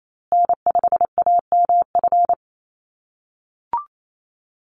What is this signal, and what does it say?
Morse code, heard as a keyed tone, about 28 wpm, 700 Hz. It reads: N5AMF
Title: USA Call Signs Repeated 3x 28wpm